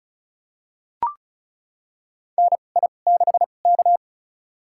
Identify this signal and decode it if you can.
Morse code, heard as a keyed tone, about 35 wpm, spaced as for 25 wpm, 700 Hz. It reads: NI6K
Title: USA Call Signs - 4 Characters - 2x Character Spacing - 35wpm